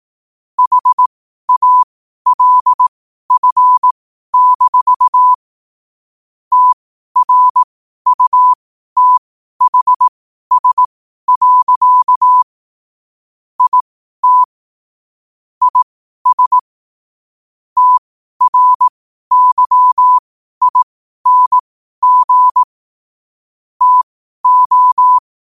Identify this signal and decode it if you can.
Morse code, heard as a keyed tone, about 18 wpm, spaced as for 12 wpm, 1000 Hz. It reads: HALF-TRUTHS. IT IS TRYING TO
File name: qotdfiles / 12wpm